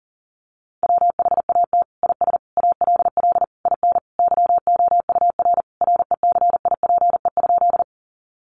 Morse code, heard as a keyed tone, about 40 wpm, 700 Hz. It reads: WHAT IS ALL IN YOUR RECIPE?
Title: Sentences from Top 1200 Words 40wpm